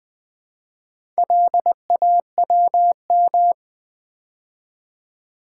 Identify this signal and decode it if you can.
Morse code, heard as a keyed tone, about 20 wpm, 700 Hz. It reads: LAWM
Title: Build A Word - 2 Letters At A Time - 8-Letter Words - 20wpm